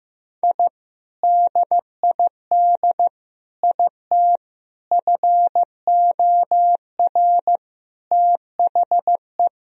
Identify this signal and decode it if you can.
Morse code, heard as a keyed tone, about 16 wpm, 700 Hz. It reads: I DID IT FOR THE
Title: Sentences from Top 1000 Words - Review Part 4 - 15wpm